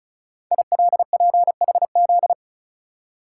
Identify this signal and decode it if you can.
Morse code, heard as a keyed tone, about 36 wpm, 690 Hz. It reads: ILPHZ
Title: Code Groups - 5 Letter:Numbers - 35wpm